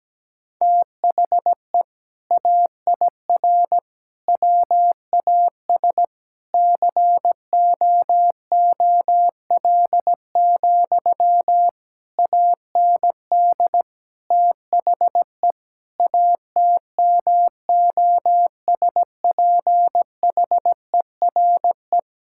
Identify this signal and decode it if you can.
Morse code, heard as a keyed tone, about 17 wpm, 700 Hz. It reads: THE AIR WAS COOL, AND THE ATMOSPHERE
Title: Sentences from Top 1600 Words 17wpm